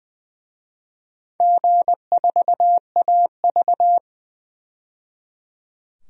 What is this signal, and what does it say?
Morse code, heard as a keyed tone, about 20 wpm, 700 Hz. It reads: G4AV
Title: DXCC-100 Call Signs - 4 Characters - 20wpm